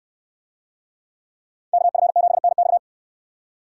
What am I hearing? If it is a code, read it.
Morse code, heard as a keyed tone, about 64 wpm, 700 Hz. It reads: KF7NX